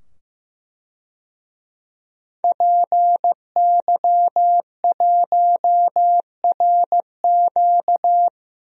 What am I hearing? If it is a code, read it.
Morse code, heard as a keyed tone, about 15 wpm, 700 Hz. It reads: PY1RQ
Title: DXCC-100 Call Signs - 5 Characters - 15wpm